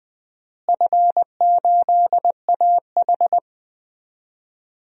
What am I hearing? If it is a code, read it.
Morse code, heard as a keyed tone, about 20 wpm, 700 Hz. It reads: F8AH
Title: DXCC-100 Call Signs - 4 Characters - Repeated 3x - 20wpm